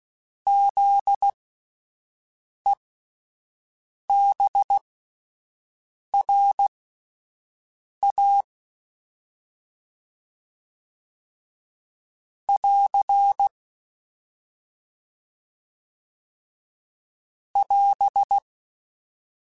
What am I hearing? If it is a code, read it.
Morse code, heard as a keyed tone, about 16 wpm, spaced as for 5 wpm, 800 Hz. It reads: ZEBRA <AR> <AS>
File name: Alphabetical / Alphabetical mp3s / Zs @5